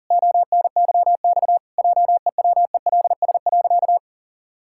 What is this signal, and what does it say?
Morse code, heard as a keyed tone, about 40 wpm, 700 Hz. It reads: ONYX JEWELS.
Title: Pangram Challenge v2 - 40wpm